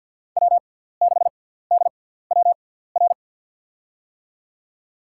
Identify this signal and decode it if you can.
Morse code, heard as a keyed tone, about 52 wpm, spaced as for 17 wpm, 700 Hz. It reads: W6DWR
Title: USA Call Signs - 5 Characters - 6x Character Spacing - 50wpm